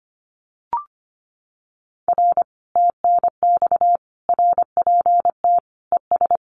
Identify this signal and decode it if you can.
Morse code, heard as a keyed tone, about 25 wpm, 700 Hz. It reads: R TNX RPT ES
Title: Common QSO Phrases Encore 25wpm